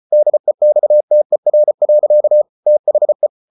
Morse code, heard as a keyed tone, about 34 wpm, 600 Hz. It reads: DEXTER. THE